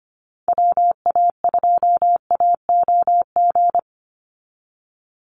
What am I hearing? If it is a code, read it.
Morse code, heard as a keyed tone, about 25 wpm, 700 Hz. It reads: WA2AOG
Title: Call Signs - 25wpm